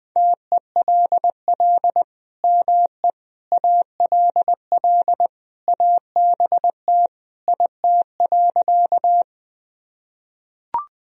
Morse code, heard as a keyed tone, about 20 wpm, 700 Hz. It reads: TELL ME ALL ABT IT.
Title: Sentences from Top 200 Words and Abbreviations 20wpm